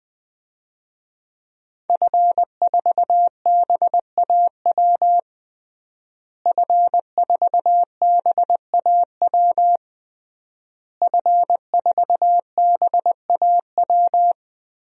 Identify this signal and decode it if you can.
Morse code, heard as a keyed tone, about 20 wpm, 700 Hz. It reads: F4BAW F4BAW F4BAW
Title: DXCC-100 Call Signs - 5 Characters - Repeated 3x - 20wpm